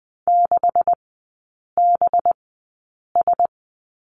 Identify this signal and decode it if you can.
Morse code, heard as a keyed tone, about 20 wpm, spaced as for 8 wpm, 700 Hz. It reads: 6BS
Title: 184 Lesson 30e - Letter b - Three Characters 20wpm